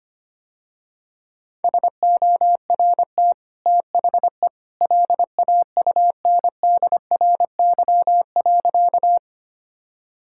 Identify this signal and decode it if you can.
Morse code, heard as a keyed tone, about 25 wpm, 700 Hz. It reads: SORT THE LAUNDRY.
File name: Easy Sentences with Hints - 25wpm